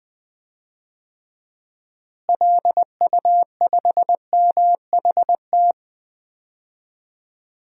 Morse code, heard as a keyed tone, about 21 wpm, 700 Hz. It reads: LU5MHT
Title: DXCC-100 Call Signs - 6 Characters - Repeated 2x - 20wpm